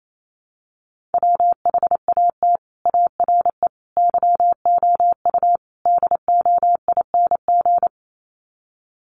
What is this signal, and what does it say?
Morse code, heard as a keyed tone, about 28 wpm, 700 Hz. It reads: WHAT ARE YOU DOING
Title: Sets of 4 Words 28wpm